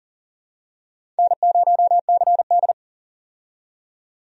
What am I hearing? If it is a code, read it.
Morse code, heard as a keyed tone, about 38 wpm, 700 Hz. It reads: N0CD
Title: USA Call Signs - 4 Characters - 40wpm